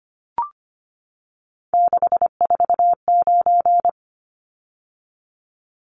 Morse code, heard as a keyed tone, about 25 wpm, 700 Hz. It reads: 649
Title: Numbers - 3-Digit 25wpm